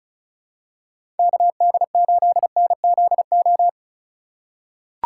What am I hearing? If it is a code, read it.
Morse code, heard as a keyed tone, about 36 wpm, 700 Hz. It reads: KD8NZO
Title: USA Call Signs - 6 Characters - 2x Character Spacing - 35wpm